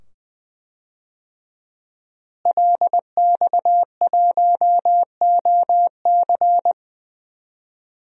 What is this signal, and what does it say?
Morse code, heard as a keyed tone, about 20 wpm, 700 Hz. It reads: LX1OC
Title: DXCC-100 Call Signs - 5 Characters - Repeated 2x - 20wpm